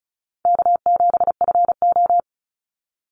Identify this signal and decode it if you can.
Morse code, heard as a keyed tone, about 34 wpm, 700 Hz. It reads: K7FO
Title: NA SOTA Chaser Callsigns 35wpm